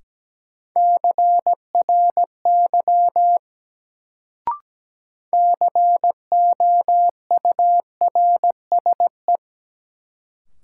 Morse code, heard as a keyed tone, about 17 wpm, 700 Hz. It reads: CRY COURSE
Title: Sentences from Top 500 Words - New Word Drill - 17wpm